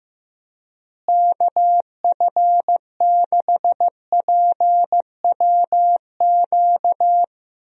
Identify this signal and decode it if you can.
Morse code, heard as a keyed tone, about 15 wpm, 700 Hz. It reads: KF6PWQ
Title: USA Call Signs - 6 Characters - Repeated 3x - 15wpm